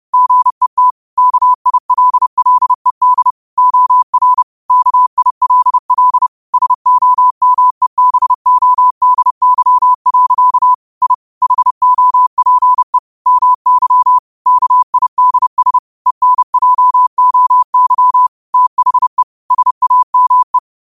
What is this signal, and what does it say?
Morse code, heard as a keyed tone, about 30 wpm, 1000 Hz. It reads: GET KILLED OR KILL SOMEBODY. I HOPE MY KIDS ENJOY THE SAME